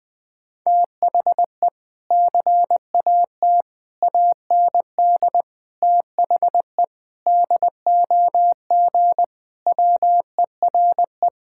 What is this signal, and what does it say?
Morse code, heard as a keyed tone, about 19 wpm, 700 Hz. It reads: THE CAT AND THE DOG WERE
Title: Sentences with 2-Word Phrases - 20wpm